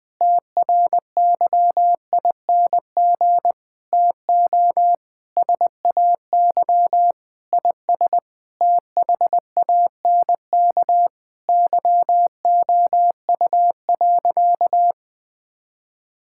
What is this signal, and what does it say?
Morse code, heard as a keyed tone, about 21 wpm, 700 Hz. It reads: TRYING TO SAY IS THANK YOU.
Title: Sentences with 8-Word Phrases - 20wpm